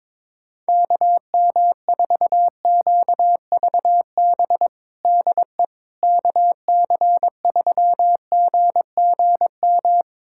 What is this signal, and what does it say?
Morse code, heard as a keyed tone, about 22 wpm, 700 Hz. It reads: KM4QVB DE KC3GGM
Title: USA Callsign DE Callsign - 6 Characters - 22wpm